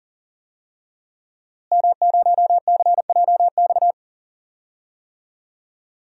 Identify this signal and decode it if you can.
Morse code, heard as a keyed tone, about 40 wpm, 700 Hz. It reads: M0CJX
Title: DXCC-100 Call Signs - 5 Characters - Repeated 3x - 40wpm